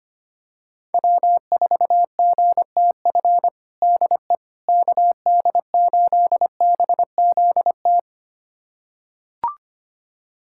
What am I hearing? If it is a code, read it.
Morse code, heard as a keyed tone, about 25 wpm, 700 Hz. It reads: W4GTF DE KD8BZT
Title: USA Callsign DE Callsign 25wpm